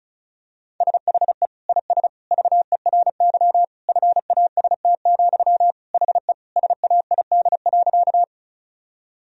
Morse code, heard as a keyed tone, about 36 wpm, 700 Hz. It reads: SHE IS VERY FAST, HE SAID.
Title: Sentences from Top 1000 Words - Review Part 2 - 35wpm